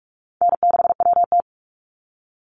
Morse code, heard as a keyed tone, about 44 wpm, 700 Hz. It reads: N6WT
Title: NA SOTA Chaser Callsigns 45wpm